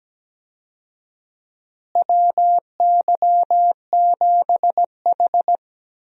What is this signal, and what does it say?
Morse code, heard as a keyed tone, about 17 wpm, 700 Hz. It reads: WY7H
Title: USA Call Signs - 4 Characters - Repeated 3x - 17wpm